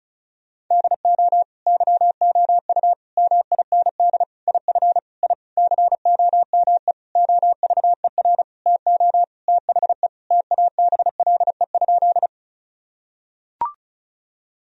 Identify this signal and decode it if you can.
Morse code, heard as a keyed tone, about 36 wpm, 700 Hz. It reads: DO YOU MIND IF I COME OVER TO THE TABLE?
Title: Sentences from Top 1000 Words - Review Part 2 - 35wpm